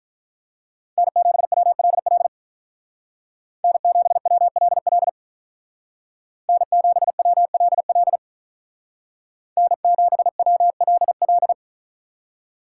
Morse code, heard as a keyed tone, about 44 wpm, 700 Hz. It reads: N7WLL N7WLL N7WLL N7WLL
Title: Speed Racing - USA 1x3 Call Signs 35wpm